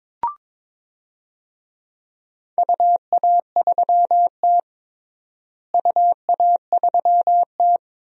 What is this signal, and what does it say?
Morse code, heard as a keyed tone, about 22 wpm, 700 Hz. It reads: UA3T UA3T
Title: DXCC-100 Call Signs - 4 Characters - Repeated 2x - 22wpm